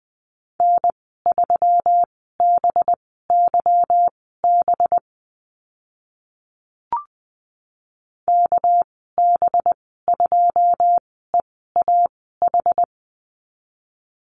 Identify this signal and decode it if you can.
Morse code, heard as a keyed tone, about 20 wpm, spaced as for 14 wpm, 700 Hz. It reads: N3BYB KB2EAH
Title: 245 Lesson 39g - Letter z - Call Signs 20wpm